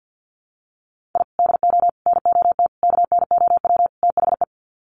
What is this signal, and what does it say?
Morse code, heard as a keyed tone, about 50 wpm, 700 Hz. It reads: I DO NOT KNOW THE
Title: Sets of 5 Words 50wpm